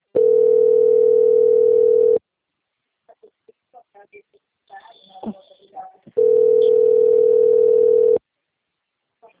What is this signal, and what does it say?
Ringback tone (North America), 440 + 480 Hz.
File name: out- - - - - .11372